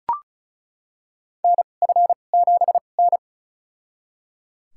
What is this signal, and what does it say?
Morse code, heard as a keyed tone, about 36 wpm, spaced as for 25 wpm, 700 Hz. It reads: NF7N
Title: USA Call Signs - 4 Characters - 2x Character Spacing - 35wpm